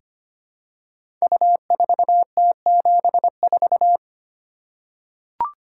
Morse code, heard as a keyed tone, about 25 wpm, 700 Hz. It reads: U4T74
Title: Code Groups - 5 Letter:Numbers - 25wpm